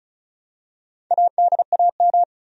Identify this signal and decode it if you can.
Morse code, heard as a keyed tone, about 36 wpm, 700 Hz. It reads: ADAM
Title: USA Common Names - 35wpm